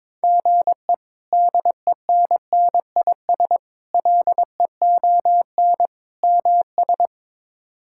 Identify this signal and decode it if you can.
Morse code, heard as a keyed tone, about 22 wpm, 700 Hz. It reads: GE DENNIS LEON MS
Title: K1USN SST QSO Practice - 22wpm